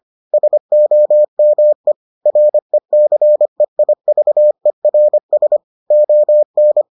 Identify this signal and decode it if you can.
Morse code, heard as a keyed tone, about 25 wpm, 600 Hz. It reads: SOME RECEIVERS ON